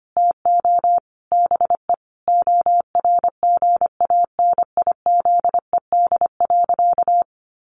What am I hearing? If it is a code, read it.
Morse code, heard as a keyed tone, about 25 wpm, 700 Hz. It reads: TO BE ORGANIZED.
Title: Sentences from Top 1300 Words 25wpm